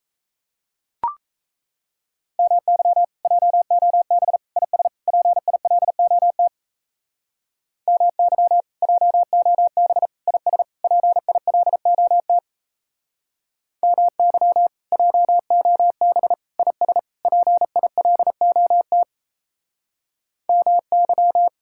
Morse code, heard as a keyed tone, about 36 wpm, 700 Hz. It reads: MY JOB IS PILOT MY JOB IS PILOT MY JOB IS PILOT MY